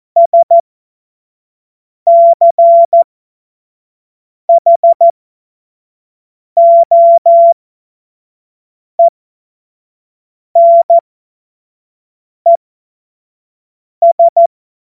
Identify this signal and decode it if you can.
Morse code, heard as a keyed tone, about 14 wpm, spaced as for 5 wpm, 680 Hz. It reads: SCHOENES
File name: Morsekurs / Witze / Witz 20. Januar 2026 25 F70